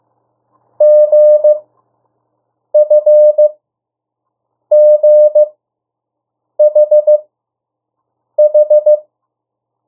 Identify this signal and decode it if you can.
Morse code, heard as a keyed tone, about 15 wpm, spaced as for 6 wpm, 600 Hz. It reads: GFGHH
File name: morse mp3 / 02 urok 1